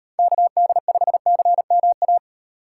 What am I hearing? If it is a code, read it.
Morse code, heard as a keyed tone, about 34 wpm, 700 Hz. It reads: KD5CMA